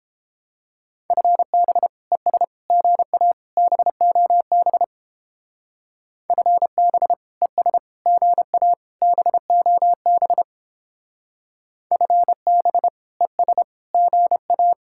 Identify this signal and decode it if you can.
Morse code, heard as a keyed tone, about 29 wpm, 700 Hz. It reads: FB ES GA BOB FB ES GA BOB FB ES GA